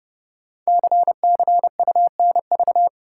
Morse code, heard as a keyed tone, about 30 wpm, 700 Hz. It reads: CCUNV
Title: Code Groups - 5 Letters - 30wpm